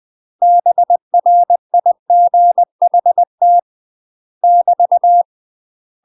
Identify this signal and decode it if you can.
Morse code, heard as a keyed tone, about 20 wpm, 700 Hz. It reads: BRIGHT <BT>
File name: UK MASTER QSOs TRAINING / 20WPM / UK MASTER QSOs 0062